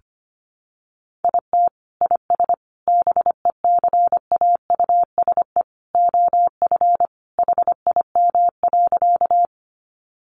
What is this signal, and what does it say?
Morse code, heard as a keyed tone, about 25 wpm, 700 Hz. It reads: IT IS BECAUSE OF HIM.
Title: Sentences from Top 600 Words 25wpm